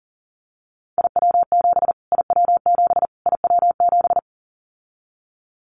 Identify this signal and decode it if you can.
Morse code, heard as a keyed tone, about 40 wpm, 700 Hz. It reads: IW7 IW7 IW7
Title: Callsign Prefixes for Top 25 DXCC - Repeated 3x - 40wpm